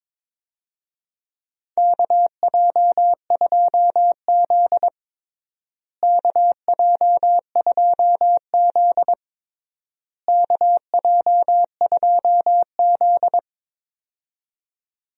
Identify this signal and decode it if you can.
Morse code, heard as a keyed tone, about 22 wpm, 700 Hz. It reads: KJ2Z KJ2Z KJ2Z
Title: USA Call Signs - 4 Characters - Repeated 3x - 22wpm